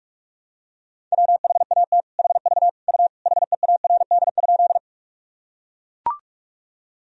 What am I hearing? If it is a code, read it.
Morse code, heard as a keyed tone, about 41 wpm, 700 Hz. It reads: WHAT HV U HEARD?
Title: Sentences from Top 400 Words and Abbreviations 45wpm